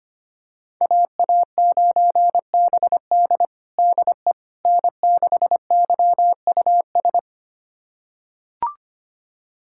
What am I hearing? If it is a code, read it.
Morse code, heard as a keyed tone, about 25 wpm, 700 Hz. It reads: AA9BD DE N6YUS
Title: USA Callsign DE Callsign 25wpm